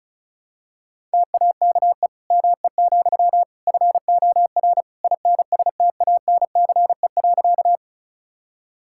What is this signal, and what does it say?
Morse code, heard as a keyed tone, about 36 wpm, 700 Hz. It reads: TAKE ME, FOR INSTANCE.
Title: Sentences from Top 1000 Words - Review Part 5 - 35wpm